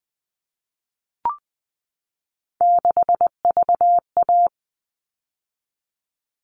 Morse code, heard as a keyed tone, about 20 wpm, 700 Hz. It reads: 6VA
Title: All Cut Numbers - 3-Digit 20wpm